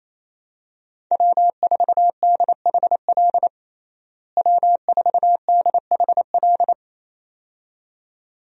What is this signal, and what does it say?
Morse code, heard as a keyed tone, about 28 wpm, 700 Hz. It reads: W4DHL W4DHL
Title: USA Call Signs - 5 Characters - Repeated 2x - 28wpm